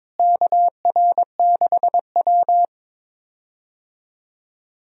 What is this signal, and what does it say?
Morse code, heard as a keyed tone, about 21 wpm, 700 Hz. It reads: KR6W
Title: USA Call Signs - 4 Characters - Repeated 2x - 22wpm